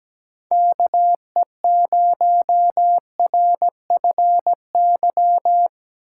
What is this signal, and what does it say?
Morse code, heard as a keyed tone, about 17 wpm, 700 Hz. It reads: KE0RFY
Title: USA Call Signs - 6 Characters - 6x Character Spacing - 17wpm